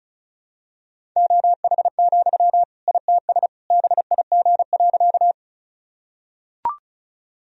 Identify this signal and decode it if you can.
Morse code, heard as a keyed tone, about 34 wpm, 700 Hz. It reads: OH, ITS BIG.
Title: Sentences from Top 1000 Words - Review Part 1 - 35wpm